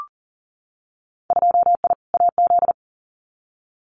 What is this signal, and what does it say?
Morse code, heard as a keyed tone, about 41 wpm, 700 Hz. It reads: 2I AZ
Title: Winter Field Day Exchanges Encore 40wpm